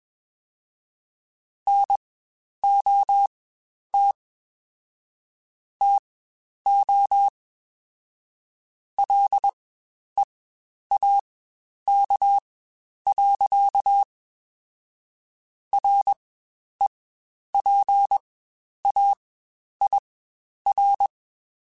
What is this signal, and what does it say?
Morse code, heard as a keyed tone, about 21 wpm, spaced as for 10 wpm, 800 Hz. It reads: NOT TO LEAK. REPAIR
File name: Quotes Spaced / Quotes Spaced-mp3s / Airtight sp @20